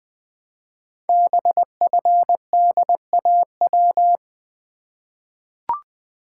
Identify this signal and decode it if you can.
Morse code, heard as a keyed tone, about 21 wpm, 700 Hz. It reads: BFDAW
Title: Code Groups - 5 Letters - 20wpm